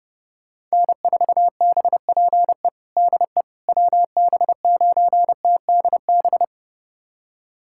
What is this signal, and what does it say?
Morse code, heard as a keyed tone, about 32 wpm, 700 Hz. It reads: N4BPE DE WB9TDB
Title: USA Callsign DE Callsign 30wpm